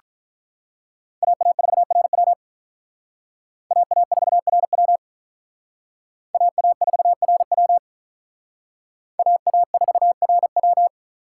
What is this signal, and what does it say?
Morse code, heard as a keyed tone, about 45 wpm, 700 Hz. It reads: AA4RW AA4RW AA4RW AA4RW